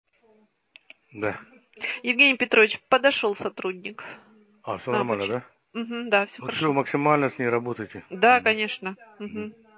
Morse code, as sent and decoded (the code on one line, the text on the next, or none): none